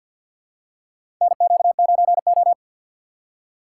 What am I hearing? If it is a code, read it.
Morse code, heard as a keyed tone, about 50 wpm, 700 Hz. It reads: NQ9O
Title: USA Call Signs - 4x Character Spacing - 50wpm